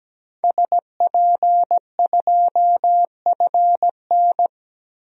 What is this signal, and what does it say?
Morse code, heard as a keyed tone, about 17 wpm, 700 Hz. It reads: SP2FN
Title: DXCC-100 Call Signs - 5 Characters - Repeated 2x - 17wpm